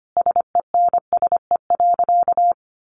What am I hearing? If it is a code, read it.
Morse code, heard as a keyed tone, about 25 wpm, 700 Hz. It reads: SENSE.
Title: Sentences from Top 1100 Words 25wpm